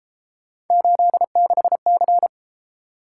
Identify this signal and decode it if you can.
Morse code, heard as a keyed tone, about 33 wpm, 700 Hz. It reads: 86C